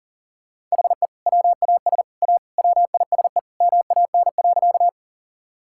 Morse code, heard as a keyed tone, about 40 wpm, 700 Hz. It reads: HE WAS A WISE MAN.
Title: Sentences from Top 1600 Words 40wpm